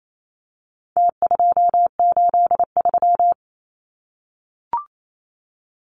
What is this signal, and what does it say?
Morse code, heard as a keyed tone, about 28 wpm, 700 Hz. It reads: T283
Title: Numbers - Contest Style - 4-Digit 28wpm